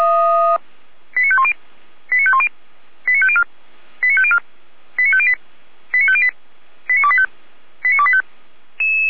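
Five-tone selective call (ZVEI), heard as quick runs of five tones, tones of about 70 ms. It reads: 89510 89510 89504 89504 89508 89508 89285 89285
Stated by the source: ZVEI five-tone selective call 89262, received on SDR, from the Feuerwehr Orlamünde audio server.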